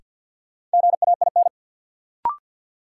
Morse code, heard as a keyed tone, about 49 wpm, 700 Hz. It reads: GAIN
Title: Sentences from Top 1000 Words - New Word Drill - 50wpm